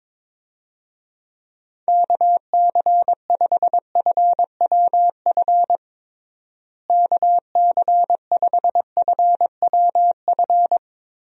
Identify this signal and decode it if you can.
Morse code, heard as a keyed tone, about 22 wpm, 700 Hz. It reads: KC5FWF KC5FWF
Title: USA Call Signs - 6 Characters - Repeated 2x - 22wpm